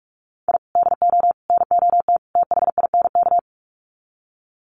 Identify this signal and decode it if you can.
Morse code, heard as a keyed tone, about 44 wpm, 700 Hz. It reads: I DO NOT THINK
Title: Sets of 4 Words 45wpm